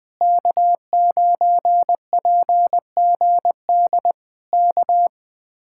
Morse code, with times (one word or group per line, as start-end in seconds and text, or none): 0.2-4.1 K9PGD
4.5-5.1 K